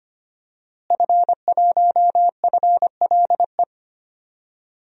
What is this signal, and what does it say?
Morse code, heard as a keyed tone, about 25 wpm, 700 Hz. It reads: F1FLE
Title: DXCC-100 Call Signs - 5 Characters - Repeated 2x - 25wpm